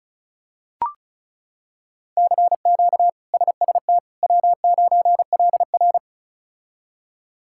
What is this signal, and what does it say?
Morse code, heard as a keyed tone, about 36 wpm, 700 Hz. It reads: CQ SST W9LR
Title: K1USN SST QSO Practice - 35wpm